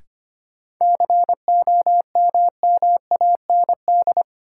Morse code, heard as a keyed tone, about 25 wpm, 700 Hz. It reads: COMMAND
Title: Sentences from Top 1600 Words - New Word Drill - 25wpm